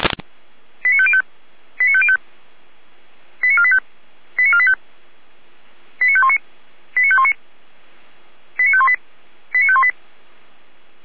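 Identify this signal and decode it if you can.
Five-tone selective call (ZVEI), heard as quick runs of five tones, tones of about 70 ms. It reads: 89505 89505 89485 89485 89519 89519 89518 89518
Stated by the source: Received on an SDR.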